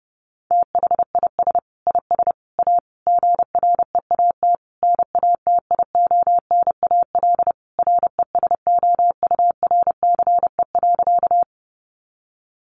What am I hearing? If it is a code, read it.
Morse code, heard as a keyed tone, about 30 wpm, 700 Hz. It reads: THIS IS A GREAT NATIONAL RESOURCE.
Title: Sentences from Top 1300 Words 30wpm